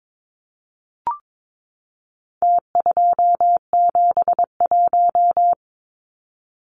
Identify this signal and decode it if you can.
Morse code, heard as a keyed tone, about 22 wpm, 700 Hz. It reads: T271
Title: Numbers - Contest Style - 4-Digit 22wpm